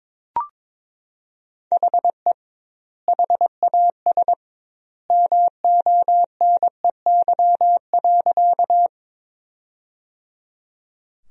Morse code, heard as a keyed tone, about 22 wpm, 700 Hz. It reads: HE HAS MONEY.
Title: 2x Word-Spacing - Sentences from Top 400 Words - 22wpm